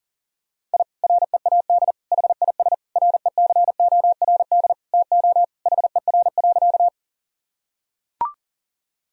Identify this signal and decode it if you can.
Morse code, heard as a keyed tone, about 40 wpm, 700 Hz. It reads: I READ HIS RECORD TO HER.
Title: Sentences from Top 1000 Words - Review Part 3 - 40wpm